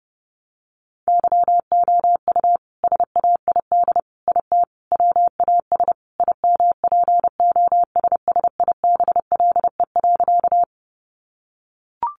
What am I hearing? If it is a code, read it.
Morse code, heard as a keyed tone, about 30 wpm, 700 Hz. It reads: YOU SAID IT WAS IMPOSSIBLE.
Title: Sentences from Top 1100 Words 30wpm